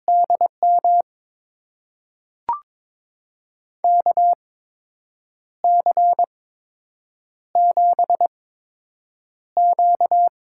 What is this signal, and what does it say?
Morse code, heard as a keyed tone, about 22 wpm, spaced as for 7 wpm, 700 Hz. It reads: DM KC7Q
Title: USA Call Signs - 6 Characters - 8x Character Spacing - 22wpm